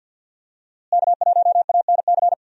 Morse code, heard as a keyed tone, about 46 wpm, 700 Hz. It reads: K1ANC
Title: USA Call Signs - 5 Characters - Repeated 2x - 50wpm